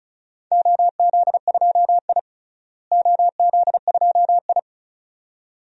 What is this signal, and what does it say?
Morse code, heard as a keyed tone, about 36 wpm, 700 Hz. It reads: OZ2I OZ2I
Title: DXCC-100 Call Signs - 4 Characters - Repeated 2x - 35wpm